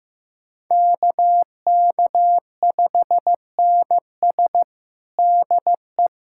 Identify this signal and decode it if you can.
Morse code, heard as a keyed tone, about 16 wpm, 700 Hz. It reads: KK5NS DE
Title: USA Callsign DE Callsign 15wpm